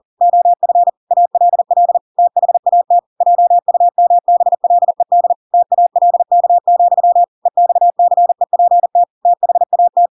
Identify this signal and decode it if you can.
Morse code, heard as a keyed tone, about 40 wpm, 700 Hz. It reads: OF ALL THAT JUMBLED TALK, EXCEPT THAT
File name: Pinocchio / Pinocchio ch7 40 40 ik1hge